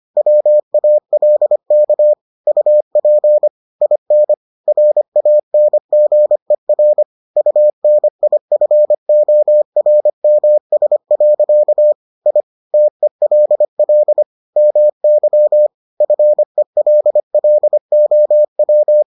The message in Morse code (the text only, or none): WALK UP IN RANGER UNIFORMS. I TELL MY FELLOW